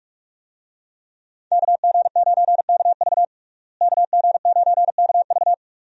Morse code, heard as a keyed tone, about 46 wpm, 700 Hz. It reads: KG9KV KG9KV
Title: USA Call Signs - 5 Characters - Repeated 2x - 45wpm